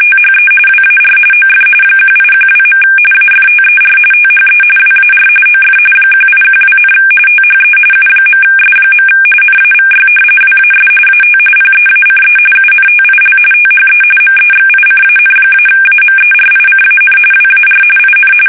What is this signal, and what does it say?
RTTY, about 45.45 baud. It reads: THE WAR OF 1812. AFTER THE WAR OF 1812, THE U.S. NAVY WAS AT PEACE UNTIL THE MEXICAN-AMERICAN WAR IN
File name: fsk850 / usnavy1